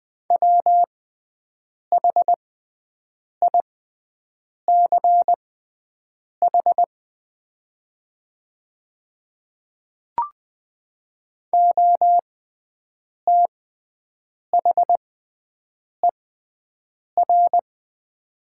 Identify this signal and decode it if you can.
Morse code, heard as a keyed tone, about 20 wpm, spaced as for 7 wpm, 700 Hz. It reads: WHICH OTHER
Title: 6x Character Spacing - Top 100 Words - 20wpm